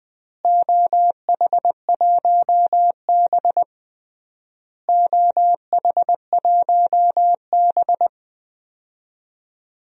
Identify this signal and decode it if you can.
Morse code, heard as a keyed tone, about 20 wpm, 700 Hz. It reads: OH1B OH1B
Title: DXCC-100 Call Signs - 4 Characters - Repeated 3x - 20wpm